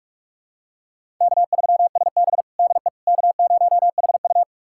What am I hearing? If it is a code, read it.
Morse code, heard as a keyed tone, about 46 wpm, 700 Hz. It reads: K3SB DE K0HU
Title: USA Callsign DE Callsign - 4 Characters - 45wpm